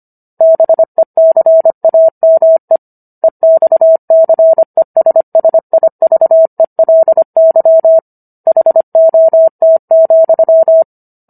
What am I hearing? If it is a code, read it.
Morse code, heard as a keyed tone, about 25 wpm, 650 Hz. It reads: BECAME EXCESSIVELY HOT,